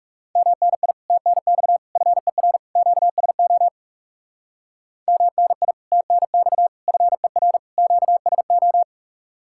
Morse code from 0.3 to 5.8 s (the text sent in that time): MNI TNX FER QSO MNI